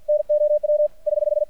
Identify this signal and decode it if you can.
Morse code, heard as a keyed tone, about 50 wpm, 600 Hz. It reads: NOW <HH>
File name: archived / CRSnet